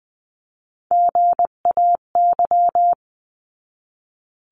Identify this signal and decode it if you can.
Morse code, heard as a keyed tone, about 20 wpm, 700 Hz. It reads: GAY
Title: 259 Lesson 42d - Words 18-20wpm